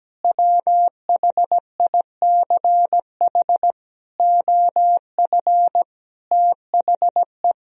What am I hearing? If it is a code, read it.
Morse code, heard as a keyed tone, about 17 wpm, 700 Hz. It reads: WHICH OF THE
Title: Sentences from Top 1000 Words - Review Part 1 - 17wpm